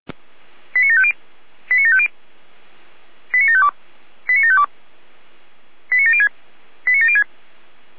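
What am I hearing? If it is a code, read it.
Five-tone selective call (ZVEI), heard as quick runs of five tones, tones of about 70 ms. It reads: 89750 89750 89752 89752 89706 89706
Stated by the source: ZVEI five-tone call 89485 received on an SDR.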